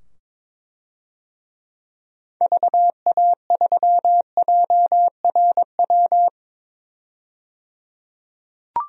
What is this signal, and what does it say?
Morse code, heard as a keyed tone, about 21 wpm, 700 Hz. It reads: VA3JRW
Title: DXCC-100 Call Signs - 6 Characters - 22wpm